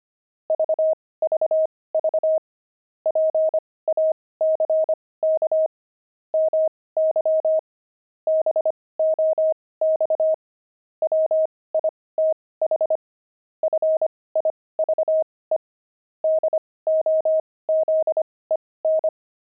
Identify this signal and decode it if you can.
Morse code, heard as a keyed tone, about 25 wpm, spaced as for 18 wpm, 630 Hz. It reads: VVV PACK MY BOX WITH FIVE DOZEN
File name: pangram-fs-18-cs-25-jugs